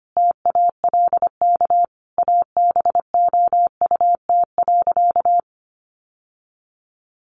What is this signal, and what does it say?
Morse code, heard as a keyed tone, about 25 wpm, 700 Hz. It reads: TALK ABOUT.
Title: Sentences from Top 800 Words 25wpm